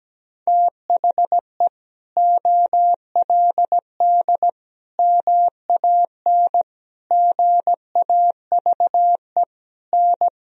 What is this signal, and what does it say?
Morse code, heard as a keyed tone, about 17 wpm, 700 Hz. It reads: THE OLD MAN GAVE N
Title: Sentences from Top 500 Words and Abbreviations 17wpm